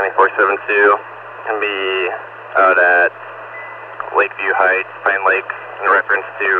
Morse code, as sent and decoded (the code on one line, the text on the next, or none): none